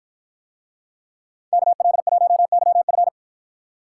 Morse code, heard as a keyed tone, about 53 wpm, 700 Hz. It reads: KL1YF